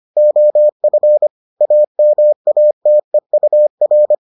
Morse code, heard as a keyed tone, about 25 wpm, 600 Hz. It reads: OF AMATEUR